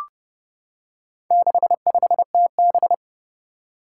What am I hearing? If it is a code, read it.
Morse code, heard as a keyed tone, about 30 wpm, 700 Hz. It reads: 65TB
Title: Four Challenging Letter-Number Combinations 30wpm